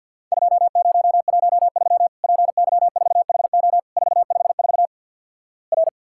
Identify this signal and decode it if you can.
Morse code, heard as a keyed tone, about 53 wpm, 700 Hz. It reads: 2013 PY4HO 454 R
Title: Worked All Europe DX Contest - QTC Exchange - 50wpm